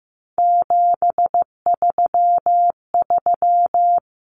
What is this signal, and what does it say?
Morse code, heard as a keyed tone, about 15 wpm, 700 Hz. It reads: 733
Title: Common Cut Numbers - 3-Digit 15wpm